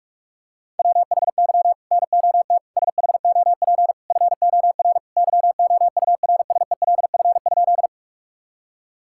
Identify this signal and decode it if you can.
Morse code, heard as a keyed tone, about 46 wpm, 700 Hz. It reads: WHY NOT SHOP FOR YOURSELF?
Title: Sentences from Top 1600 Words 45wpm